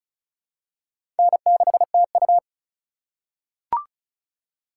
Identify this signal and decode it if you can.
Morse code, heard as a keyed tone, about 36 wpm, 700 Hz. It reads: N6TU
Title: USA Call Signs - 4 Characters - 3x Character Spacing - 35wpm